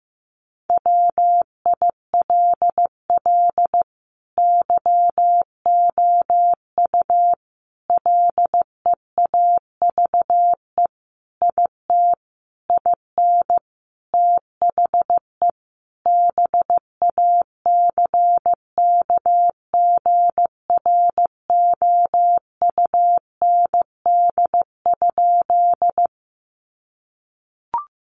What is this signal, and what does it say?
Morse code, heard as a keyed tone, about 15 wpm, 700 Hz. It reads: WILL YOU LEAVE IT IN THE BACKGROUND?
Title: Sentences from Top 1300 Words 15wpm